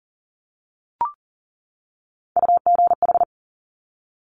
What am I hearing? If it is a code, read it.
Morse code, heard as a keyed tone, about 38 wpm, 700 Hz. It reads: UGH
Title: Top Three Letter Combinations in Words 40wpm